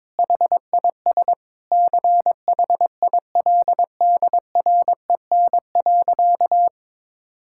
Morse code, heard as a keyed tone, about 22 wpm, 700 Hz. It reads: HIS CHILDREN.
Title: Sentences from Top 1600 Words 22wpm